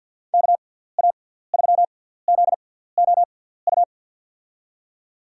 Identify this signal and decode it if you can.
Morse code, heard as a keyed tone, about 53 wpm, spaced as for 17 wpm, 700 Hz. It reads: KA3ZOU
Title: USA Call Signs - 6 Characters - 6x Character Spacing - 50wpm